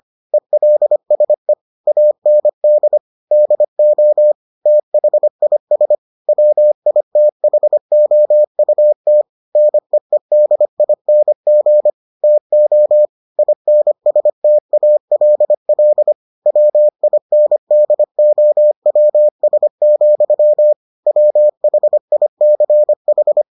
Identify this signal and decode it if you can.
Morse code, heard as a keyed tone, about 25 wpm, 600 Hz. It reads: ELSE AND DO THIS WITHOUT NEEDING TO INSTALL WINDOWS, WHICH